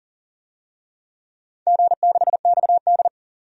Morse code, heard as a keyed tone, about 40 wpm, 700 Hz. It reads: G6XD
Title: DXCC-100 Call Signs - 4 Characters - 40wpm